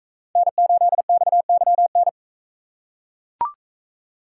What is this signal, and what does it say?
Morse code, heard as a keyed tone, about 40 wpm, 700 Hz. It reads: N8XYN